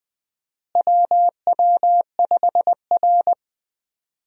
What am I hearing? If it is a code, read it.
Morse code, heard as a keyed tone, about 20 wpm, 700 Hz. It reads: WW5R
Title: USA Call Signs - 4 Characters - Repeated 2x - 20wpm